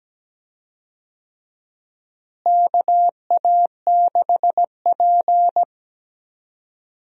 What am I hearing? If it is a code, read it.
Morse code, heard as a keyed tone, about 17 wpm, 700 Hz. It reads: KA6P
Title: USA Call Signs - 4 Characters - Repeated 2x - 17wpm